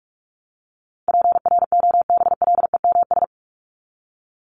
Morse code, heard as a keyed tone, about 44 wpm, 700 Hz. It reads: PROBLEMS
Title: ES:S Suffix - Top 100 Words with Base Words 45wpm